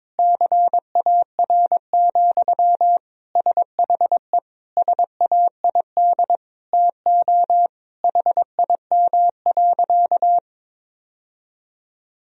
Morse code, heard as a keyed tone, about 22 wpm, 700 Hz. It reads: CAR, SHE SAID TO HIM.
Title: Sentences from Top 1000 Words - Review Part 1 - 22wpm